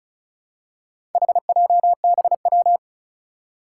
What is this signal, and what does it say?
Morse code, heard as a keyed tone, about 36 wpm, 690 Hz. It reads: HJBW
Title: Four Challenging Letter-Number Combinations 35wpm